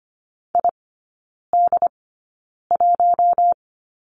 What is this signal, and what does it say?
Morse code, heard as a keyed tone, about 25 wpm, spaced as for 9 wpm, 700 Hz. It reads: ID1
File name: Lesson 15e - Number 5 - Three Characters 25wpm